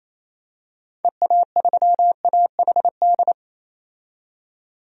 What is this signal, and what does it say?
Morse code, heard as a keyed tone, about 28 wpm, 700 Hz. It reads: EA3AHD
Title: DXCC-100 Call Signs - 6 Characters - 28wpm